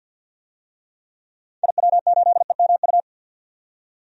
Morse code, heard as a keyed tone, about 50 wpm, 700 Hz. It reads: IW8EMU
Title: DXCC-100 Call Signs - 6 Characters - 50wpm